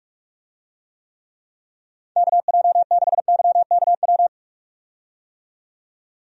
Morse code, heard as a keyed tone, about 46 wpm, 700 Hz. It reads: KJ6YKW
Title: USA Call Signs - 6 Characters - 45wpm